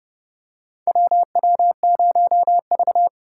Morse code, heard as a keyed tone, about 30 wpm, 700 Hz. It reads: WW0V
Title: USA Call Signs - 3x Character Spacing - 30wpm